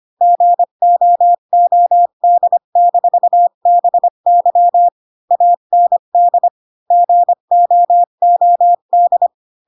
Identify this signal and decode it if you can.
Morse code, heard as a keyed tone, about 25 wpm, 700 Hz. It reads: GOOD-BY AND GOOD
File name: Pinocchio / Pinocchio ch5 20 25 ik1hge